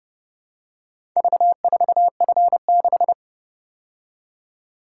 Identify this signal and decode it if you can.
Morse code, heard as a keyed tone, about 31 wpm, 700 Hz. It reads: V4F6